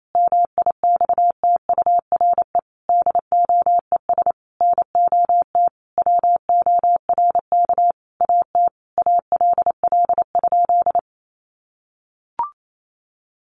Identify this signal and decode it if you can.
Morse code, heard as a keyed tone, about 28 wpm, 700 Hz. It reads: MIXTURE DOES NOT WORK AT ALL?
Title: Sentences from Top 1500 Words 28wpm